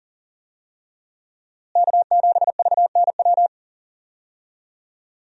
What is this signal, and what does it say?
Morse code, heard as a keyed tone, about 40 wpm, 700 Hz. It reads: K7VNW
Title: USA Call Signs - 5 Characters - Repeated 2x - 40wpm